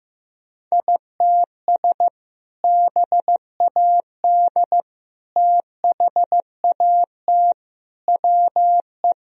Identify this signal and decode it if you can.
Morse code, heard as a keyed tone, about 15 wpm, 700 Hz. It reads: ITS BAD THAT WE
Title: Sentences from Top 1000 Words - Review Part 1 - 15wpm